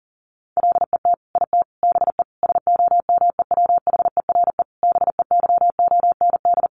Morse code, heard as a keyed tone, about 40 wpm, 700 Hz. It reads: LET IT BE SOMEWHERE BEYOND
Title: Sentences from Top 1400 Words 40wpm